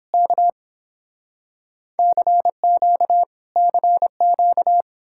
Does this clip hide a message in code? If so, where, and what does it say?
Yes: Morse code from 0.1 to 4.8 s, heard as a keyed tone, about 30 wpm, 700 Hz. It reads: K CQ CQ